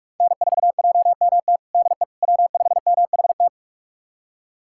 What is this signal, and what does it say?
Morse code, heard as a keyed tone, about 46 wpm, 700 Hz. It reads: N4JMT DE W5MHT
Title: USA Callsign DE Callsign - 5 Characters - 45wpm